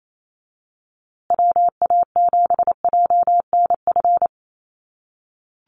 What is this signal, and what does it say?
Morse code, heard as a keyed tone, about 28 wpm, 700 Hz. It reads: WA7JNF
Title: USA Call Signs Repeated 3x 28wpm